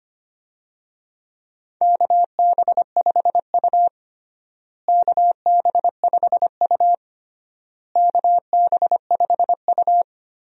Morse code, heard as a keyed tone, about 25 wpm, 700 Hz. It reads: KB5U KB5U KB5U
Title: USA Call Signs - 4 Characters - Repeated 3x - 25wpm